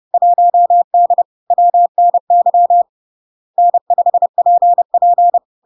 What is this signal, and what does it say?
Morse code, heard as a keyed tone, about 30 wpm, 700 Hz. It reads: 1D WNY N5PP
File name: LARK-FD-2022-Log-in-CW-30wpm-700-Hz